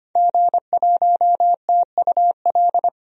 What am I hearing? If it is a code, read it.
Morse code, heard as a keyed tone, about 25 wpm, 700 Hz. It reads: G1TUL
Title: DXCC-100 Call Signs - 5 Characters - Repeated 3x - 25wpm